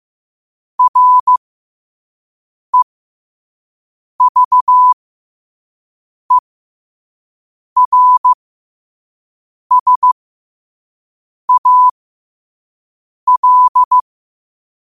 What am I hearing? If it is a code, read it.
Morse code, heard as a keyed tone, about 15 wpm, spaced as for 5 wpm, 1000 Hz. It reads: REVERSAL